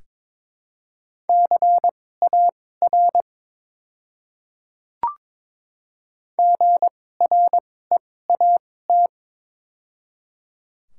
Morse code, heard as a keyed tone, about 22 wpm, spaced as for 16 wpm, 700 Hz. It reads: CAR GREAT
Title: 2x Character Spacing - Top 300 Words - 22wpm